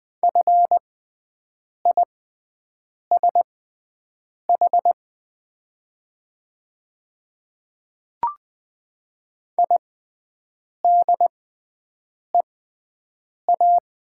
Morse code, heard as a keyed tone, about 20 wpm, spaced as for 7 wpm, 700 Hz. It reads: FISH IDEA
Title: 6x Character Spacing - Top 500 Words - 20wpm